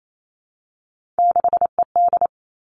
Morse code, heard as a keyed tone, about 28 wpm, 700 Hz. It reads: 6ED
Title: Three Challenging Letter-Number Combinations 28wpm